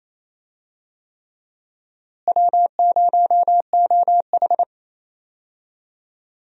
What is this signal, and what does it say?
Morse code, heard as a keyed tone, about 28 wpm, 700 Hz. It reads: W0OH
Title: USA Call Signs - 4 Characters - Repeated 3x - 28wpm